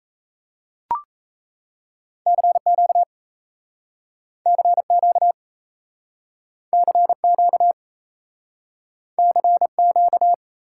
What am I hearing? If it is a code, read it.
Morse code, heard as a keyed tone, about 36 wpm, 700 Hz. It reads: CQ CQ CQ CQ